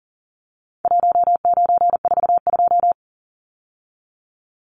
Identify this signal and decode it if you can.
Morse code, heard as a keyed tone, about 40 wpm, 700 Hz. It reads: 1942
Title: Numbers - Contest Style - 4-Digit 40wpm